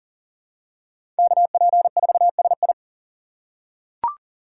Morse code, heard as a keyed tone, about 40 wpm, 700 Hz. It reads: KP4SI
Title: USA Call Signs - 5 Characters - 6x Character Spacing - 40wpm